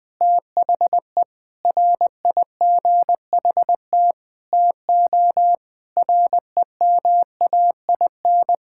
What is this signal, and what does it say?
Morse code, heard as a keyed tone, about 21 wpm, 700 Hz. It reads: THE RIGHT TO REMAIN
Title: Sentences with 4-Word Phrases - 20wpm